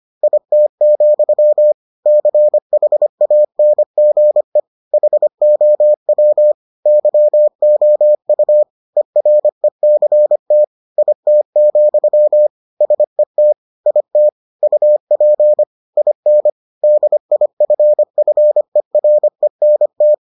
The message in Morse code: IT, CHANGE HOW YOU ERECT IT, SET IT UP IN DIFFERENT

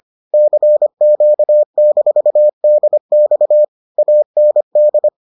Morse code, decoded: CQ-DX AND

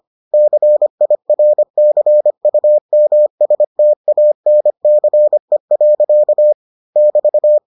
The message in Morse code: CIRCUMSTANCE. <BT>

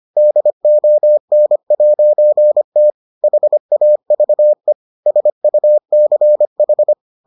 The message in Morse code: DON'T HAVE SUCH

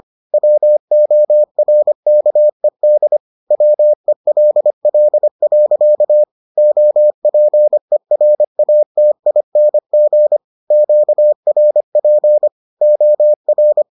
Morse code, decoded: WORKED WELL. OPERATING QRP OR

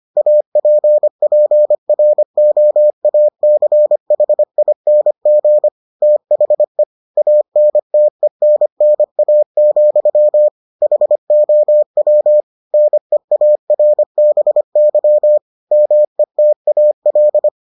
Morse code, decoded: APPROACHING THE ANTENNA, HOW NEARBY METAL

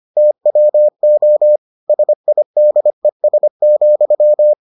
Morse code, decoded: TWO SIDES,

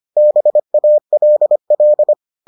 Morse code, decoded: BALL

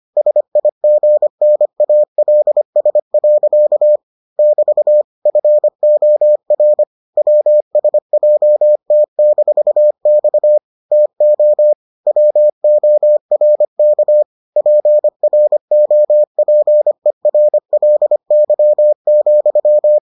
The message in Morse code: SIGNALS. <BT> FOR WSJT-X TO WORK PROPERLY,

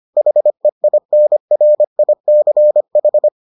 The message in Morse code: HEINRICH

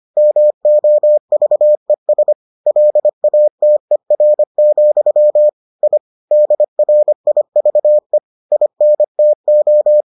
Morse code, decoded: MOVES LATER, I DRIVE INTO

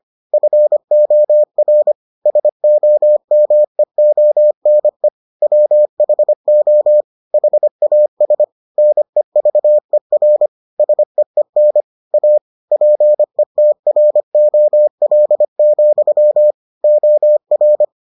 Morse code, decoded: FOR SOMEONE WHO HAS NEVER SEEN A PETROL, OR

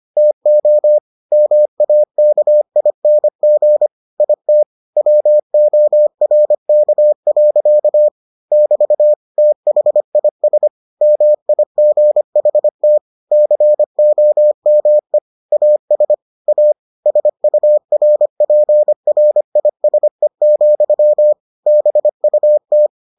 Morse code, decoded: TO MAKING IT WORK. <BT> THIS MIGHT COME AS A SURPRISE, BUT

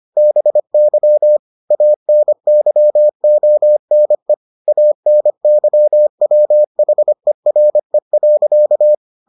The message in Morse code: BY ANYONE ANYWHERE.

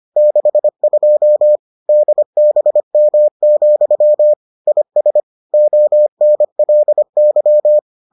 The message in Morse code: 62 DBM, IS ONLY

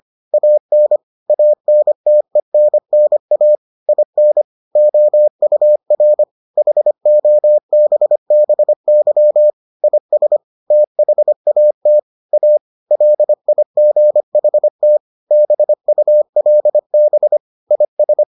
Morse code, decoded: AN ANTENNA IN OUR HOBBY IS THAT A LIGHT BULB IS

